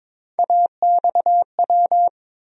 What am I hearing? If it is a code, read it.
Morse code, heard as a keyed tone, about 22 wpm, 700 Hz. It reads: AXW
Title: USA Call Sign Postfixes - 3 Letters 22wpm